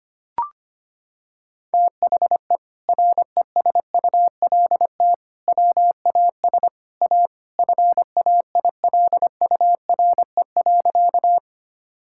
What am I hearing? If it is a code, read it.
Morse code, heard as a keyed tone, about 25 wpm, 700 Hz. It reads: THE RESULT WAS A FAILURE.
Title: Sentences from Top 1000 Words - Review Part 6 - 25wpm